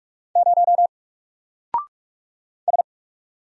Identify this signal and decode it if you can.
Morse code, heard as a keyed tone, about 48 wpm, 700 Hz. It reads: 0 S
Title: Single Letter-Number Alt 45wpm